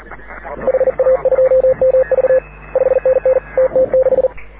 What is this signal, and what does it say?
Morse code, heard as a keyed tone, about 39 wpm, 530 Hz. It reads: HA2MV 5NN TT6
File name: AUDIO2019 / HG6N CW SSB MIX